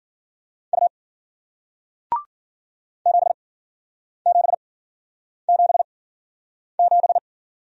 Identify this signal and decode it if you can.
Morse code, heard as a keyed tone, about 53 wpm, spaced as for 9 wpm, 700 Hz. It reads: U 7777